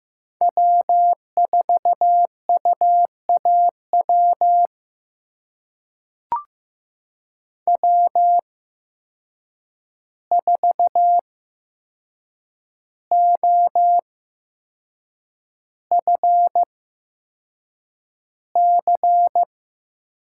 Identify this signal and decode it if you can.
Morse code, heard as a keyed tone, about 15 wpm, spaced as for 6 wpm, 700 Hz. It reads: W4UAW W4OFC
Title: USA Call Signs - 8x Character Spacing - 15wpm